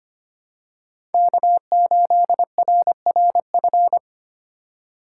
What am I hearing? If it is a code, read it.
Morse code, heard as a keyed tone, about 25 wpm, 700 Hz. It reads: K8RRF
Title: USA Call Signs - 8x Character Spacing - 25wpm